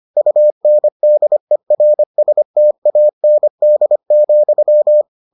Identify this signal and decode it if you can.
Morse code, heard as a keyed tone, about 25 wpm, 600 Hz. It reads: UNDERSTAND,